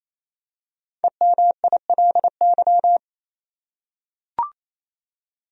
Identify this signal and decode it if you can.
Morse code, heard as a keyed tone, about 28 wpm, 700 Hz. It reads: EMILY
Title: USA Common Names - 28wpm